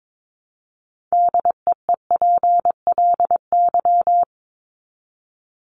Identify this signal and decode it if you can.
Morse code, heard as a keyed tone, about 22 wpm, 700 Hz. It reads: DEEPLY
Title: Top 1751-2000 Words 22wpm